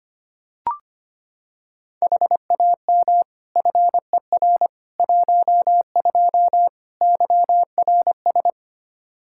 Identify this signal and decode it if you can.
Morse code, heard as a keyed tone, about 25 wpm, 700 Hz. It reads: HAM FER 12 YRS
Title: Practice QSO - Exchange 3 - More Info - 25wpm